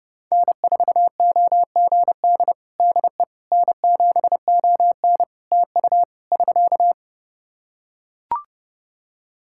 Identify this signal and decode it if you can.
Morse code, heard as a keyed tone, about 28 wpm, 700 Hz. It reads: N4OGD DE N7ON TU <SK>
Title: Practice QSO - Exchange 4 - Ending - 30wpm